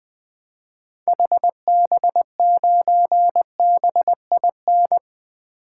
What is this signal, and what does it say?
Morse code, heard as a keyed tone, about 20 wpm, 700 Hz. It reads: HB9BIN
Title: 2023 SOTA Activator Call Signs - 20wpm